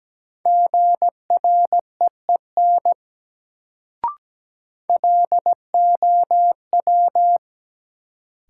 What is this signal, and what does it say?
Morse code, heard as a keyed tone, about 17 wpm, 700 Hz. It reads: GREEN LOW
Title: Sentences from Top 500 Words - New Word Drill - 17wpm